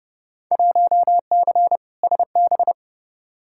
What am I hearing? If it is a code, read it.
Morse code, heard as a keyed tone, about 30 wpm, 700 Hz. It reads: 1C SB
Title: Field Day Exchanges Encore - 30wpm